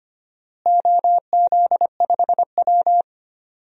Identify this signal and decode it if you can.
Morse code, heard as a keyed tone, about 25 wpm, 700 Hz. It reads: OZ5W
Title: DXCC-100 Call Signs - 4 Characters - Repeated 2x - 25wpm